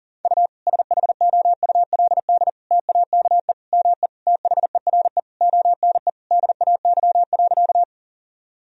Morse code, heard as a keyed tone, about 40 wpm, 700 Hz. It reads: U SHOULD TAKE ME THERE ONE DAY.
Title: Sentences from Top 300 Words and Abbreviations 40wpm